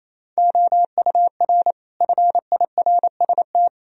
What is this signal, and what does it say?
Morse code, heard as a keyed tone, about 28 wpm, 700 Hz. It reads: OUR FIRST
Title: Sentences from Top 1000 Words - Review Part 3 - 28wpm